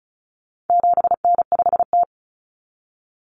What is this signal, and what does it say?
Morse code, heard as a keyed tone, about 34 wpm, 700 Hz. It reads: 7N5T
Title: Common Cut Numbers - 4-Digit 35wpm